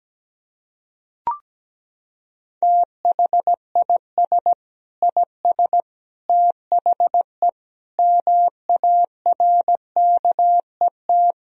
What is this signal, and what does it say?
Morse code, heard as a keyed tone, about 17 wpm, 700 Hz. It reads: THIS IS THE MARKET
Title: Sentences from Top 1000 Words - Review Part 3 - 17wpm